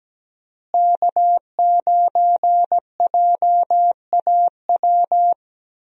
Morse code, heard as a keyed tone, about 17 wpm, 700 Hz. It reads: K9JAW
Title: USA Call Signs - 5 Characters - Repeated 3x - 17wpm